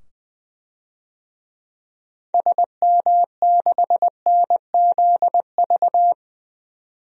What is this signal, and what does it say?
Morse code, heard as a keyed tone, about 20 wpm, 700 Hz. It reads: SM6NZV
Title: DXCC-100 Call Signs - 6 Characters - 20wpm